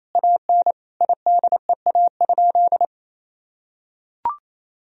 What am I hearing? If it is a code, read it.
Morse code, heard as a keyed tone, about 27 wpm, 700 Hz. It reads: AN IDEA?
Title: Sentences from Top 400 Words and Abbreviations 28wpm